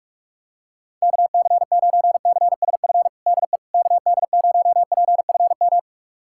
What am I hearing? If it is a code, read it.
Morse code, heard as a keyed tone, about 46 wpm, 700 Hz. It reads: KC9CSF DE KD0PFM
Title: USA Callsign DE Callsign - 6 Characters - 45wpm